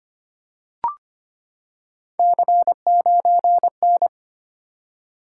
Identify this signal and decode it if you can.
Morse code, heard as a keyed tone, about 25 wpm, 700 Hz. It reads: C9N
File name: Three Letter-Number Combinations - 25wpm